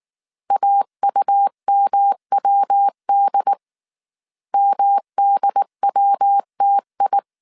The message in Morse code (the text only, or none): AUMWD MDWTI